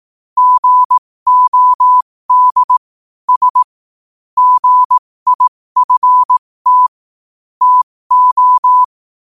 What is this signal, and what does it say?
Morse code, heard as a keyed tone, about 18 wpm, spaced as for 15 wpm, 1000 Hz. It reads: GOD S GIFT TO
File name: qotdfiles / 15wpm